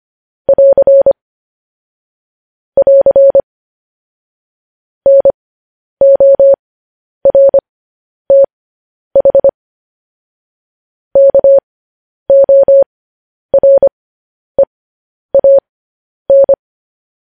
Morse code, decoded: <AR> <AR> NORTH KOREAN

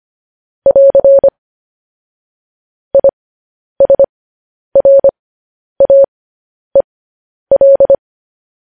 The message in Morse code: <AR> ISRAEL

